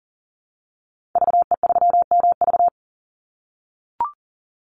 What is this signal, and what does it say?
Morse code, heard as a keyed tone, about 40 wpm, 700 Hz. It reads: VE3MV
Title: Call Signs - 40wpm